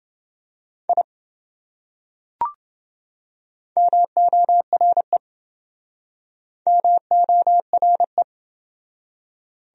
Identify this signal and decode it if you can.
Morse code, heard as a keyed tone, about 25 wpm, 700 Hz. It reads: I MORE MORE